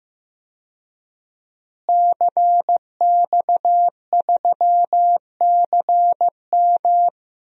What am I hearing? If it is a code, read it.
Morse code, heard as a keyed tone, about 15 wpm, 700 Hz. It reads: CX3CM
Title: DXCC-100 Call Signs - 5 Characters - Repeated 3x - 15wpm